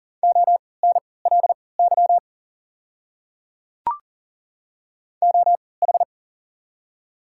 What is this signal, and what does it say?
Morse code, heard as a keyed tone, about 40 wpm, spaced as for 23 wpm, 700 Hz. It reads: ONLY OH
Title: 3x Character Spacing - Top 500 Words - 40wpm